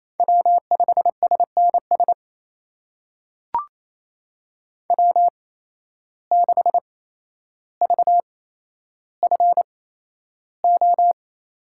Morse code, heard as a keyed tone, about 28 wpm, spaced as for 12 wpm, 700 Hz. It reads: W5SNS W6VFO
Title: USA Call Signs - 5 Characters - 8x Character Spacing - 28wpm